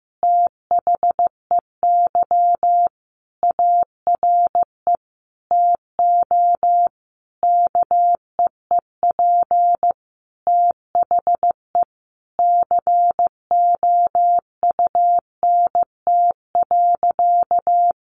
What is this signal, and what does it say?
Morse code, heard as a keyed tone, about 15 wpm, 700 Hz. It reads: THEY ARE TO KEEP THE COUNT.
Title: Sentences from Top 1500 Words 15wpm